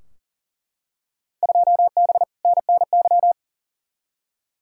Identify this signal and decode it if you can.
Morse code, heard as a keyed tone, about 40 wpm, 700 Hz. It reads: 2B NNY
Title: Field Day Exchanges Encore - 40wpm